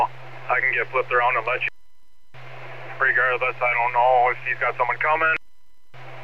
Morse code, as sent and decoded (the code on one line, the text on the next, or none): none